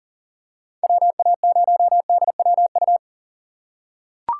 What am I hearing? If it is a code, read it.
Morse code, heard as a keyed tone, about 40 wpm, 700 Hz. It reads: WA0DWU
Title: USA Call Signs - 6 Characters - 3x Character Spacing - 40wpm